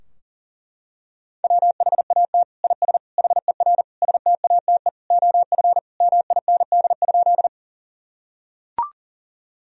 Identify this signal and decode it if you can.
Morse code, heard as a keyed tone, about 40 wpm, 700 Hz. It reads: WHAT IS HER STATE OF MIND?
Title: Sentences from Top 1000 Words - Review Part 2 - 40wpm